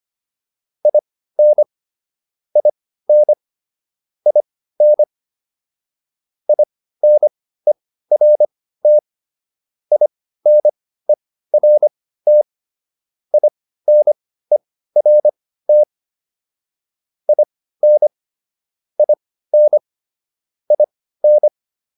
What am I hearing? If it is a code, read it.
Morse code, heard as a keyed tone, about 25 wpm, spaced as for 15 wpm, 600 Hz. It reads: IN IN IN INERT INERT INERT IN IN IN
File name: IN5 15